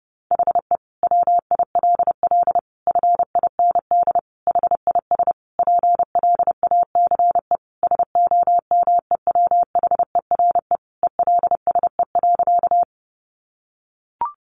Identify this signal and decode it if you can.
Morse code, heard as a keyed tone, about 28 wpm, 700 Hz. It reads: HE WILL FIND HIS PLACE SOMEWHERE ELSE.
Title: Sentences from Top 1400 Words 30wpm